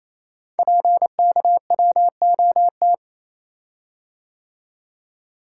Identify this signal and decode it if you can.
Morse code, heard as a keyed tone, about 28 wpm, 700 Hz. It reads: PKWOT
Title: Code Groups - 5 Letters - 28wpm